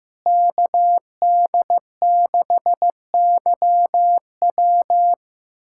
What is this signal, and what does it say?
Morse code, heard as a keyed tone, about 16 wpm, 700 Hz. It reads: KD6YW
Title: USA Callsign DE Callsign - 5 Characters - 15wpm